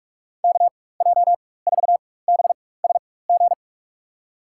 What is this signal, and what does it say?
Morse code, heard as a keyed tone, about 42 wpm, spaced as for 20 wpm, 700 Hz. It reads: KJ4BSG
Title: USA Call Signs - 6 Characters - 4x Character Spacing - 45wpm